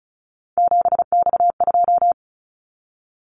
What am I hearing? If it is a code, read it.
Morse code, heard as a keyed tone, about 36 wpm, 700 Hz. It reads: 7X2
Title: Callsign Prefixes for Top 100 DXCC - 35wpm